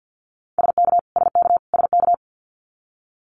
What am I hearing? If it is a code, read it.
Morse code, heard as a keyed tone, about 50 wpm, 700 Hz. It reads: SK SK SK
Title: Canadian Provinces and Territories - Repeated 3x - 50wpm